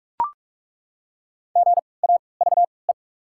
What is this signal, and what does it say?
Morse code, heard as a keyed tone, about 42 wpm, spaced as for 25 wpm, 690 Hz. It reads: GAVE
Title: 3x Character Spacing - Top 500 Words - 45wpm